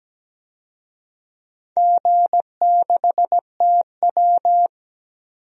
Morse code, heard as a keyed tone, about 17 wpm, 700 Hz. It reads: G6TW
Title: DXCC-100 Call Signs - 4 Characters - 17wpm